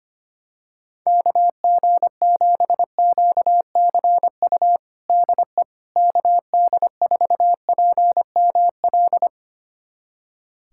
Morse code, heard as a keyed tone, about 25 wpm, 700 Hz. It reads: KG7QCU DE KD4PML
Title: USA Callsign DE Callsign - 6 Characters - 25wpm